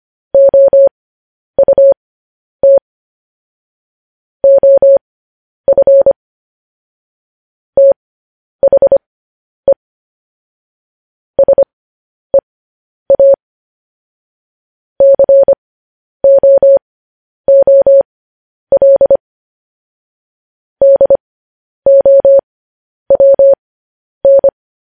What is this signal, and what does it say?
Morse code, heard as a keyed tone, about 25 wpm, spaced as for 10 wpm, 550 Hz. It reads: OUT OF THE SEA COOL DOWN
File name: podcast / News headlines-1x-25x10